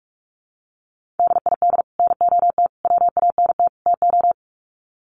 Morse code, heard as a keyed tone, about 44 wpm, 700 Hz. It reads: DID NOT WANT TO
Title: Sets of 4 Words 45wpm